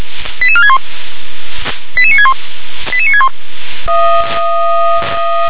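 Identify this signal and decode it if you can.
Five-tone selective call (ZVEI), heard as quick runs of five tones, tones of about 70 ms. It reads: 88461 88961 88961
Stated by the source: ZVEI five-tone selective call received by SDR.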